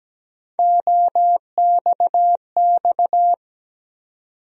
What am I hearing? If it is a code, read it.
Morse code, heard as a keyed tone, about 17 wpm, 700 Hz. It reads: OXX
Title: USA Call Sign Postfixes - 3 Letters 17wpm